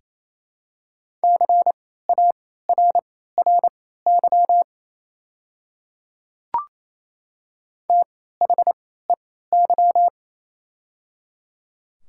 Morse code, heard as a keyed tone, about 28 wpm, spaced as for 16 wpm, 700 Hz. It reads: CARRY THEY
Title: 3x Character Spacing - Top 300 Words - 28wpm